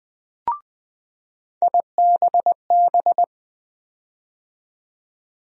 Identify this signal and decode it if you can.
Morse code, heard as a keyed tone, about 20 wpm, 700 Hz. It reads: IBB
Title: USA Call Sign Postfixes - 3 Letters 20wpm